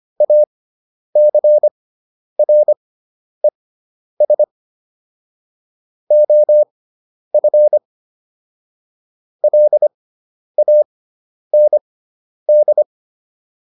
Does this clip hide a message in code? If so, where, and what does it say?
Yes: Morse code from 0.2 to 12.8 s, heard as a keyed tone, about 25 wpm, spaced as for 10 wpm, 600 Hz. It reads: ACRES OF LAND